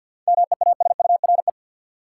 Morse code, heard as a keyed tone, about 52 wpm, 700 Hz. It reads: MEASURE
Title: Sentences from Top 1000 Words - Review Part 1 - New Word Drill - 50wpm